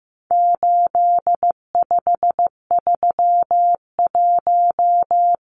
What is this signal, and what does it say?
Morse code, heard as a keyed tone, about 15 wpm, 700 Hz. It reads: 8531
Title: Numbers - 4-Digit 15wpm